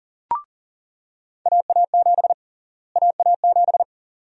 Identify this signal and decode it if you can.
Morse code, heard as a keyed tone, about 40 wpm, 700 Hz. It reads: AA7 AA7
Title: USA Call Sign Prefixes - Repeated 2x - 40wpm